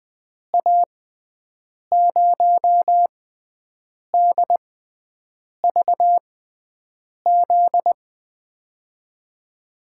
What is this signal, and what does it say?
Morse code, heard as a keyed tone, about 20 wpm, spaced as for 7 wpm, 700 Hz. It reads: A0DVZ
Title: USA Call Signs - 6 Characters - 6x Character Spacing - 20wpm